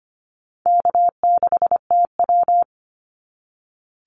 Morse code, heard as a keyed tone, about 25 wpm, 700 Hz. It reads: K6TW
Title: NA SOTA Activator Callsigns 25wpm